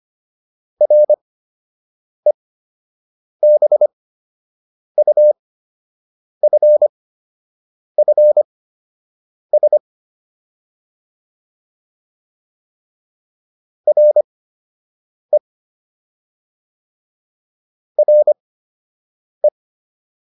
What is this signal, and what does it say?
Morse code, heard as a keyed tone, about 25 wpm, spaced as for 7 wpm, 600 Hz. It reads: REBUFFS RE RE